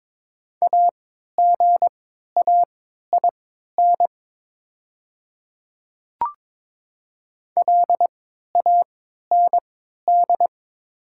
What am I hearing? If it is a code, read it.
Morse code, heard as a keyed tone, about 22 wpm, spaced as for 13 wpm, 690 Hz. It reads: AGAIN LAND
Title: 3x Character Spacing - Top 200 Words - 22wpm